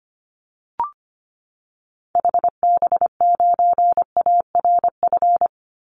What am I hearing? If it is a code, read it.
Morse code, heard as a keyed tone, about 25 wpm, 700 Hz. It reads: HB9ARF
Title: Call Signs - 25wpm